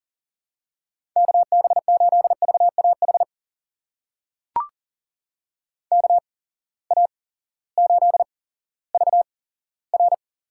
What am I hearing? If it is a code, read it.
Morse code, heard as a keyed tone, about 40 wpm, spaced as for 18 wpm, 700 Hz. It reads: KB8VAH KA8VR
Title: USA Call Signs - 6 Characters - 8x Character Spacing - 40wpm